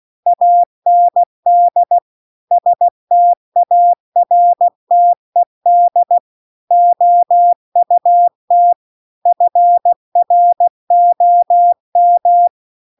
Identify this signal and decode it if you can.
Morse code, heard as a keyed tone, about 16 wpm, 700 Hz. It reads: AND STARTED OUT FROM